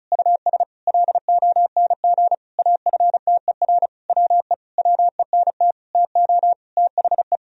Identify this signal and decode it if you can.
Morse code, heard as a keyed tone, about 36 wpm, 700 Hz. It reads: US LONG AFTER WE WENT TO THE
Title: Sentences from Top 1000 Words - Review Part 2 - 35wpm